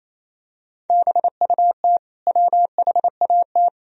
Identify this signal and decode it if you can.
Morse code, heard as a keyed tone, about 28 wpm, 700 Hz. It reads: BUT WHAT
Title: Sentences from Top 1000 Words - Review Part 3 - 28wpm